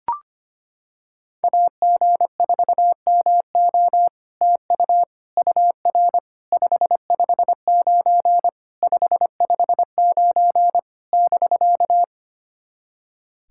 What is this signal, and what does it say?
Morse code, heard as a keyed tone, about 25 wpm, 700 Hz. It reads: AG4MO TU UR 559 559 <BK>
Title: SOTA QSO Practice 25wpm